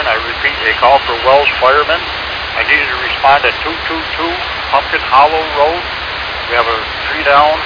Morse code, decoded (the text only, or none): none